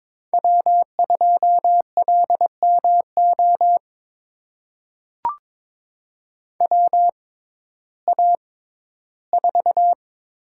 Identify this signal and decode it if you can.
Morse code, heard as a keyed tone, about 22 wpm, spaced as for 13 wpm, 700 Hz. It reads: W2LMO WA4
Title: USA Call Signs - 5 Characters - 6x Character Spacing - 22wpm